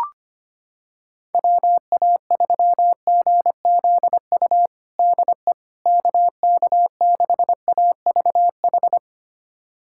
Morse code, heard as a keyed tone, about 25 wpm, 700 Hz. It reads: WA3GZU DE KK6AVH
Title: USA Callsign DE Callsign - 6 Characters - 25wpm